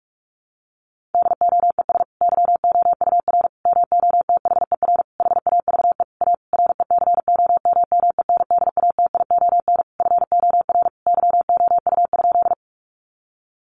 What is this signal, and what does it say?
Morse code, heard as a keyed tone, about 44 wpm, 700 Hz. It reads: DOES YOUR MOTHER HAVE A RECOMMENDATION FOR YOU?
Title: Sentences from Top 1400 Words 45wpm